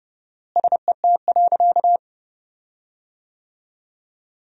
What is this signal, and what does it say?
Morse code, heard as a keyed tone, about 32 wpm, 700 Hz. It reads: SET.
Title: 4x Word-Spacing - Sentences from Top 400 Words - 30wpm